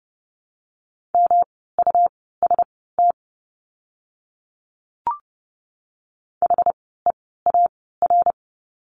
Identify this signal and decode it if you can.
Morse code, heard as a keyed tone, about 30 wpm, spaced as for 17 wpm, 700 Hz. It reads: MUST HEAR
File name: course / 30:10wpm / 237 Lesson 38f - Letter q - Words 30wpm